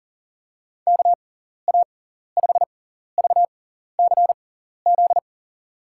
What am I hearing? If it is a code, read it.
Morse code, heard as a keyed tone, about 40 wpm, spaced as for 14 wpm, 700 Hz. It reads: KA5VCZ
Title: USA Call Signs - 6x Character Spacing - 40wpm